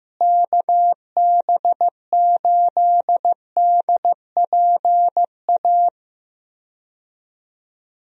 Morse code, heard as a keyed tone, about 15 wpm, 700 Hz. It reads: KB8DPA
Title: USA Call Signs - 6 Characters - Repeated 2x - 15wpm